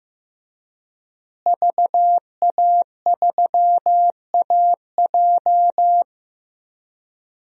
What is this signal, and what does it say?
Morse code, heard as a keyed tone, about 15 wpm, 700 Hz. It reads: VA3AJ
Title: DXCC-100 Call Signs - 5 Characters - Repeated 3x - 15wpm